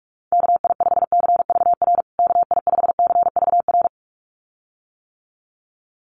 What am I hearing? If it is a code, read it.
Morse code, heard as a keyed tone, about 44 wpm, 700 Hz. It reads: KI5CVR KI5CVR
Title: USA Call Signs Repeated 2x 45wpm